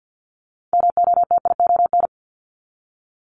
Morse code, heard as a keyed tone, about 50 wpm, 700 Hz. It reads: MOTION
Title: Multi-letter Phonemes Repeated 3x 50wpm